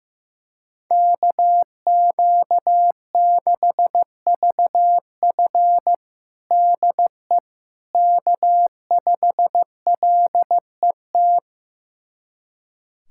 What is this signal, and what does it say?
Morse code, heard as a keyed tone, about 15 wpm, 700 Hz. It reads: KQ6VF DE K5LET
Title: USA Callsign DE Callsign - 5 Characters - 15wpm